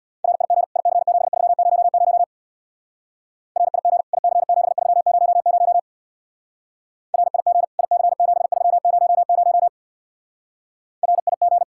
Morse code, heard as a keyed tone, about 65 wpm, 700 Hz. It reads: RIG IC7300 RIG IC7300 RIG IC7300 RIG